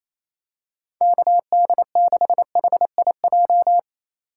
Morse code, heard as a keyed tone, about 28 wpm, 700 Hz. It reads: KD6HIJ
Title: USA Call Signs - 6 Characters - 2x Character Spacing - 28wpm